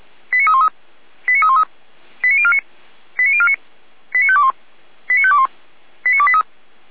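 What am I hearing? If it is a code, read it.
Five-tone selective call (ZVEI), heard as quick runs of five tones, tones of about 70 ms. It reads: 89313 89313 89049 89049 89521 89521 89283
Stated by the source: ZVEI five-tone selective call received on SDR, recorded by Feuerwehr Orlamünde.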